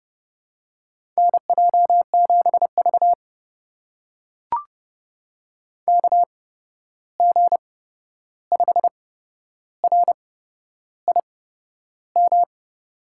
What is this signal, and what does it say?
Morse code, heard as a keyed tone, about 30 wpm, spaced as for 11 wpm, 700 Hz. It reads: NJ7V KG5RIM
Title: USA Call Signs - 8x Character Spacing - 30wpm